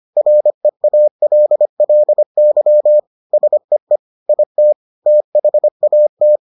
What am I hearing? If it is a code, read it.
Morse code, heard as a keyed tone, about 25 wpm, 600 Hz. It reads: REALLY SEE IT THAT